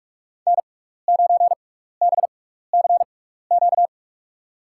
Morse code, heard as a keyed tone, about 46 wpm, spaced as for 15 wpm, 700 Hz. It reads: N9BCQ
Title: USA Call Signs - 5 Characters - 6x Character Spacing - 45wpm